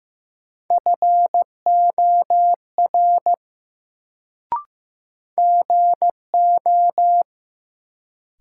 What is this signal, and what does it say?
Morse code, heard as a keyed tone, about 15 wpm, 700 Hz. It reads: FOR GO
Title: Sets of 2 Words - New Word Drill - 15wpm